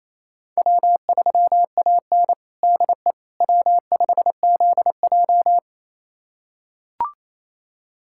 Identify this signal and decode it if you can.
Morse code, heard as a keyed tone, about 27 wpm, 700 Hz. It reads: W3AN DE W5ZJ
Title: USA Callsign DE Callsign - 4 Characters - 28wpm